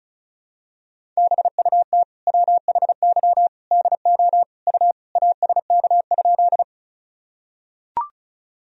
Morse code, heard as a keyed tone, about 36 wpm, 700 Hz. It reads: BUT WHY DO U ASK?
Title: Sentences from Top 200 Words and Abbreviations 35wpm